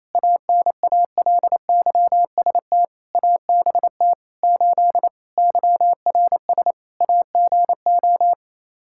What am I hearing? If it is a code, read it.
Morse code, heard as a keyed tone, about 27 wpm, 700 Hz. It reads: ANALYST ABT 8 YRS AGO
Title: Practice QSO - Exchange 3 - More Info - 28wpm